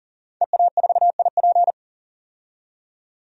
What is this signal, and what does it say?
Morse code, heard as a keyed tone, about 40 wpm, 700 Hz. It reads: EA4IP
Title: DXCC-100 Call Signs - 5 Characters - 40wpm